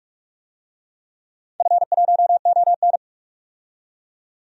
Morse code, heard as a keyed tone, about 41 wpm, 700 Hz. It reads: F1ON
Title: DXCC-100 Call Signs - 4 Characters - 45wpm